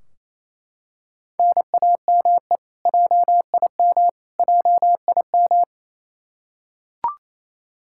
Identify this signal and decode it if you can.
Morse code, heard as a keyed tone, about 28 wpm, 700 Hz. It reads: NAME JIM JIM
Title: Practice QSO - Exchange 1 - Beginning - 28wpm